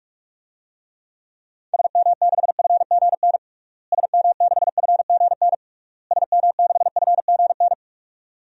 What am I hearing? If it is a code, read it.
Morse code, heard as a keyed tone, about 46 wpm, 700 Hz. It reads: SM6FGN SM6FGN SM6FGN
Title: DXCC-100 Call Signs - 6 Characters - Repeated 3x - 45wpm